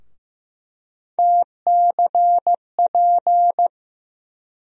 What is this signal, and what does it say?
Morse code, heard as a keyed tone, about 15 wpm, 700 Hz. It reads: TCP
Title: USA Call Sign Postfixes - 3 Letters 15wpm